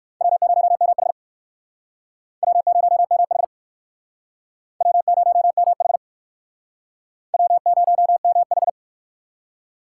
Morse code, heard as a keyed tone, about 56 wpm, 700 Hz. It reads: W0MH W0MH W0MH W0MH
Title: Speed Racing - USA 1x2 Call Signs 45wpm